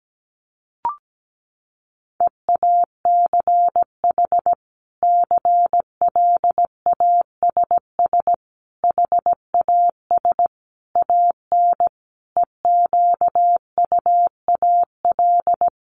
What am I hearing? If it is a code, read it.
Morse code, heard as a keyed tone, about 17 wpm, 700 Hz. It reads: EACH CLASS HAS AN EQUAL
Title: Sentences from Top 1400 Words 17wpm